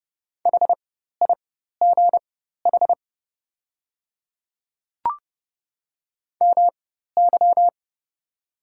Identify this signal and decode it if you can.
Morse code, heard as a keyed tone, about 30 wpm, spaced as for 14 wpm, 700 Hz. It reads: HIGH MY
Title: 4x Character Spacing - Top 200 Words - 30wpm